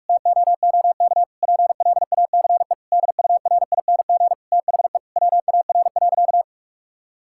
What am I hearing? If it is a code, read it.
Morse code, heard as a keyed tone, about 46 wpm, 700 Hz. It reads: TOOK PLACE DURING THE WAR.
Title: Sentences from Top 1000 Words - Review Part 6 - 45wpm